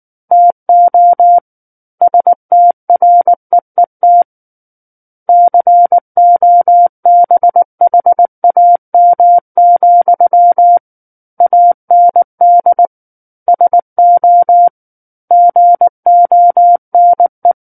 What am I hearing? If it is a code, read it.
Morse code, heard as a keyed tone, about 19 wpm, 700 Hz. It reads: TO STREET COBHAM, AND SO GONE
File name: Book One / 10-Chapter 10 - 19 WPM